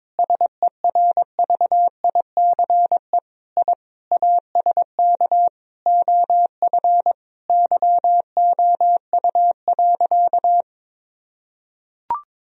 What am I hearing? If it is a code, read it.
Morse code, heard as a keyed tone, about 22 wpm, 700 Hz. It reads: SERVICE I ASK OF YOU.
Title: Sentences from Top 1000 Words - Review Part 4 - 22wpm